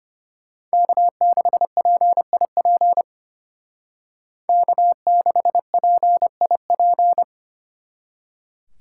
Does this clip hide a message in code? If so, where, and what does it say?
Yes: Morse code from 0.7 to 7.2 s, heard as a keyed tone, about 33 wpm, 700 Hz. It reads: K6PIP K6PIP